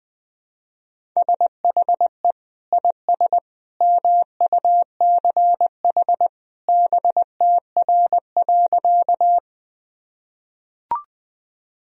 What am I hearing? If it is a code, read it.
Morse code, heard as a keyed tone, about 20 wpm, 700 Hz. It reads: SHE IS MUCH BTR.
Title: Sentences from Top 400 Words and Abbreviations 20wpm